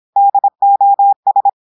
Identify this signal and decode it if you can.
Morse code, heard as a keyed tone, about 26 wpm, 800 Hz. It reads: DOS
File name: audios / portugais / portugais texte en clair 1 vitesse 26 mots minutes